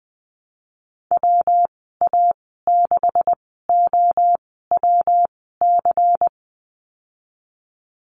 Morse code, heard as a keyed tone, about 20 wpm, spaced as for 14 wpm, 700 Hz. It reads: WA6OWC
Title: 213 Lesson 34g - Number 8 - Call Signs 20wpm